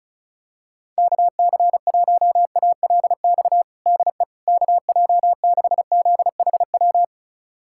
Morse code, heard as a keyed tone, about 36 wpm, 700 Hz. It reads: KC1ALX DE KJ6ZHW
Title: USA Callsign DE Callsign - 6 Characters - 35wpm